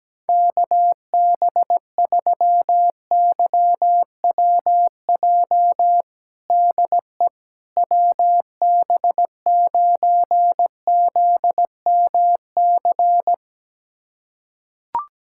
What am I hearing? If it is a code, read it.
Morse code, heard as a keyed tone, about 17 wpm, 700 Hz. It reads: KB3YWJ DE WB9ZMC
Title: USA Callsign DE Callsign 17wpm